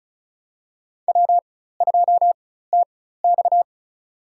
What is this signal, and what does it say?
Morse code, heard as a keyed tone, about 36 wpm, spaced as for 16 wpm, 700 Hz. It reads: W2TX
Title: USA Call Signs - 4 Characters - 4x Character Spacing - 35wpm